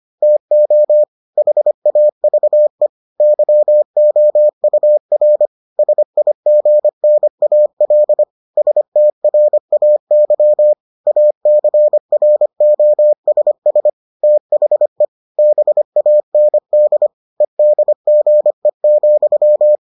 Morse code, heard as a keyed tone, about 25 wpm, 600 Hz. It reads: TO HAVE YOUR SIGNAL STRAY ACROSS THE BAND EDGE,